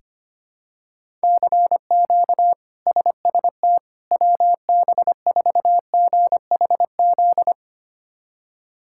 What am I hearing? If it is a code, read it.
Morse code, heard as a keyed tone, about 25 wpm, 700 Hz. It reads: CQ SST WB4GHZ
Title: K1USN SST QSO Practice - 25wpm